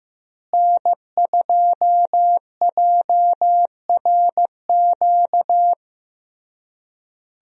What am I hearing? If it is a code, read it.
Morse code, heard as a keyed tone, about 15 wpm, 700 Hz. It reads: N2JRQ
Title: USA Call Signs - 5 Characters - Repeated 3x - 15wpm